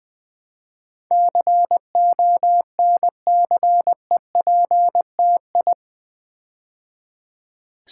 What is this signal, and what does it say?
Morse code, heard as a keyed tone, about 20 wpm, 700 Hz. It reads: CONCEPTI
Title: Build A Word - 4 Letters At A Time - 10-Letter Words - 20wpm